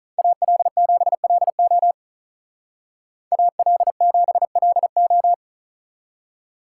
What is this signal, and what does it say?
Morse code, heard as a keyed tone, about 43 wpm, 700 Hz. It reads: AL7LO AL7LO
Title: Speed Racing - USA 2x2 Call Signs 35wpm